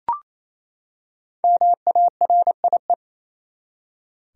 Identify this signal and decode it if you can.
Morse code, heard as a keyed tone, about 28 wpm, 700 Hz. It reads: MARIE
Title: USA Common Names - Max 5 Letters - 28wpm